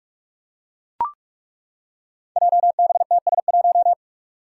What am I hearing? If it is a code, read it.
Morse code, heard as a keyed tone, about 42 wpm, 700 Hz. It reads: JBTS1
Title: Five Challenging Letter-Number Combinations 45wpm